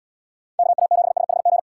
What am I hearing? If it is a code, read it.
Morse code, heard as a keyed tone, about 68 wpm, 700 Hz. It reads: KA7SHG